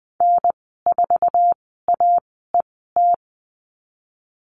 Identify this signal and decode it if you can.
Morse code, heard as a keyed tone, about 20 wpm, spaced as for 15 wpm, 700 Hz. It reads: N4AET
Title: 055 Lesson 10g - Letter r - Call Signs 20wpm